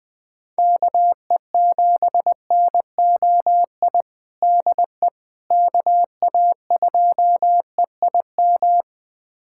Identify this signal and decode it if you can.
Morse code, heard as a keyed tone, about 21 wpm, 700 Hz. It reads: KE7NOI DE KA2EIM
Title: USA Callsign DE Callsign - 6 Characters - 20wpm